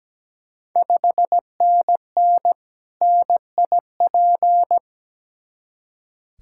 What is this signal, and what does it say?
Morse code, heard as a keyed tone, about 17 wpm, 700 Hz. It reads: 5NN NIP
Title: Ontario QSO Party Exchange 17wpm